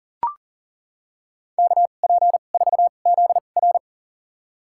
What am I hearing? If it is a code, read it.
Morse code, heard as a keyed tone, about 40 wpm, spaced as for 29 wpm, 700 Hz. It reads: KP4ZR
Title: USA Call Signs - 5 Characters - 2x Character Spacing - 40wpm